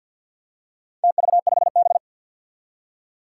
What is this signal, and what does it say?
Morse code, heard as a keyed tone, about 51 wpm, 700 Hz. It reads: TV5B
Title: Four Challenging Letter-Number Combinations 50wpm